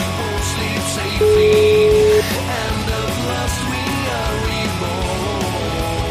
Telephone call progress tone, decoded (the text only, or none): ringback tone